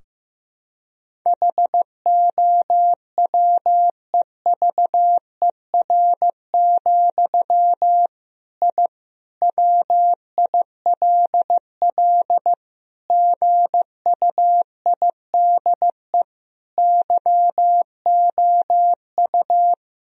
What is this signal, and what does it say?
Morse code, heard as a keyed tone, about 15 wpm, 700 Hz. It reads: HOWEVER, I WILL GUIDE YOU
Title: Sentences from Top 1000 Words - Review Part 3 - 15wpm